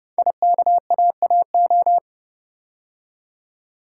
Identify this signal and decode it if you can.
Morse code, heard as a keyed tone, about 31 wpm, 690 Hz. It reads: IKAAO
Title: Code Groups - 5 Letters - 30wpm